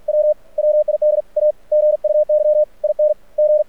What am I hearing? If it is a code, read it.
Morse code, heard as a keyed tone, about 48 wpm, 600 Hz. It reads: W WID A GUY IN G
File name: archived / CRSnet